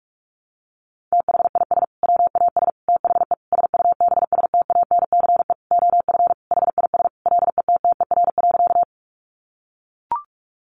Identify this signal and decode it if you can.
Morse code, heard as a keyed tone, about 44 wpm, 700 Hz. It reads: THIS WAS THE SUBSTANCE OF HIS LETTER.
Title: Sentences from Top 1500 Words 45wpm